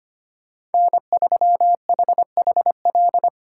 Morse code, heard as a keyed tone, about 32 wpm, 700 Hz. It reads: N3HHL